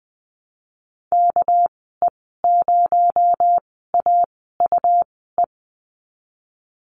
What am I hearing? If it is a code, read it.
Morse code, heard as a keyed tone, about 20 wpm, 700 Hz. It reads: K E 0 A U E
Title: 238 Lesson 38g - Letter q - Call Signs 20wpm